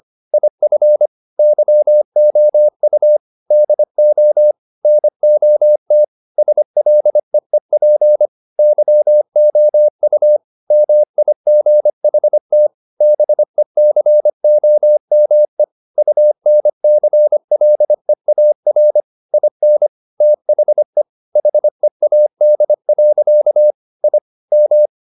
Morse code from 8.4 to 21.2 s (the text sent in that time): YOU MIGHT BECOME UNCLEAR IN THE